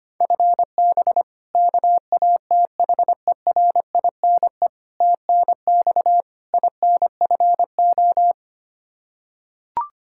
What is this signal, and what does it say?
Morse code, heard as a keyed tone, about 25 wpm, 700 Hz. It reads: FB KATHERINE TNX INFO
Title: K1USN SST QSO Practice 25wpm